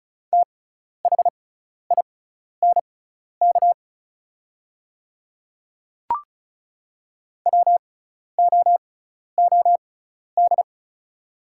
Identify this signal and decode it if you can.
Morse code, heard as a keyed tone, about 35 wpm, spaced as for 12 wpm, 690 Hz. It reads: THINK WOOD
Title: 6x Character Spacing - Top 300 Words - 35wpm